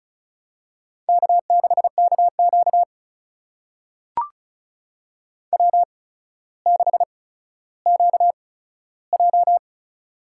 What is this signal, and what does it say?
Morse code, heard as a keyed tone, about 36 wpm, spaced as for 15 wpm, 700 Hz. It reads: K6KQ W6QJ
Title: USA Call Signs - 4 Characters - 8x Character Spacing - 35wpm